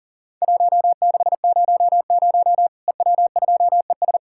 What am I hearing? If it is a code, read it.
Morse code, heard as a keyed tone, about 40 wpm, 700 Hz. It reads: 1600 EW2ES
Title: Worked All Europe DX Contest - QTC Exchange - 40wpm